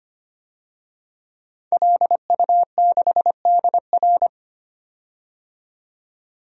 Morse code, heard as a keyed tone, about 25 wpm, 700 Hz. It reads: LU6DR
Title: DXCC-100 Call Signs - 5 Characters - 25wpm